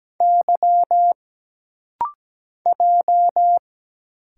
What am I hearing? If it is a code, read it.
Morse code, heard as a keyed tone, about 17 wpm, 700 Hz. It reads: Y J
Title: Single Letters - Rapid-Fire 17wpm